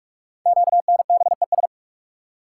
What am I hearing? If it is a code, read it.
Morse code, heard as a keyed tone, about 44 wpm, 700 Hz. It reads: QNBES
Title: Code Groups - 5 Letters - 45wpm